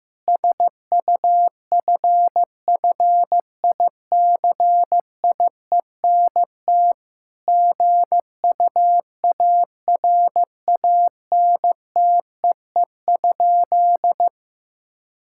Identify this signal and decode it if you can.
Morse code, heard as a keyed tone, about 16 wpm, 700 Hz. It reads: SUFFICIENT GUARANTEE?
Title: Sentences from Top 1600 Words 15wpm